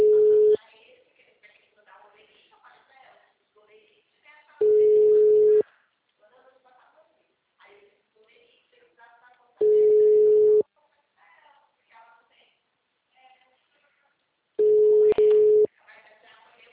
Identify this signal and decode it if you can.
Ringback tone (Europe, 425 Hz), with speech.